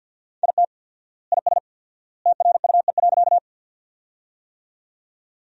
Morse code, heard as a keyed tone, about 52 wpm, 700 Hz. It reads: IT IS TRUE.
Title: 4x Word-Spacing - Sentences from Top 400 Words - 50wpm